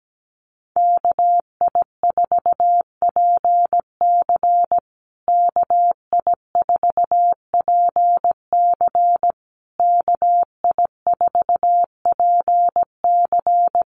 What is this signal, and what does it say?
Morse code, heard as a keyed tone, about 17 wpm, 700 Hz. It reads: KI4PC KI4PC KI4PC
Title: USA Call Signs Repeated 3x 17wpm